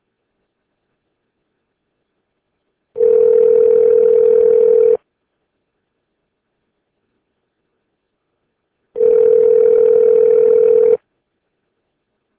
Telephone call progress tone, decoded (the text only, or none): ringback tone